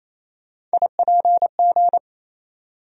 Morse code, heard as a keyed tone, about 28 wpm, 700 Hz. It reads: IPG
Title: USA Call Sign Postfixes - 3 Letters 28wpm